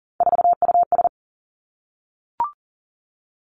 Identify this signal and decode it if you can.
Morse code, heard as a keyed tone, about 38 wpm, 700 Hz. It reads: 4US
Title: Three Challenging Letter-Number Combinations 40wpm